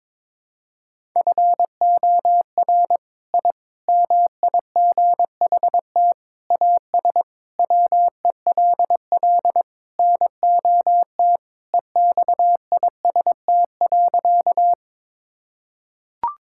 Morse code, heard as a keyed tone, about 22 wpm, 700 Hz. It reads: FOR I MIGHT AS WELL NOT EXIST.
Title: Sentences from Top 1000 Words - Review Part 5 - 22wpm